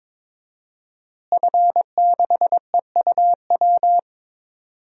Morse code, heard as a keyed tone, about 22 wpm, 700 Hz. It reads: F6EUW
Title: DXCC-100 Call Signs - 5 Characters - Repeated 2x - 22wpm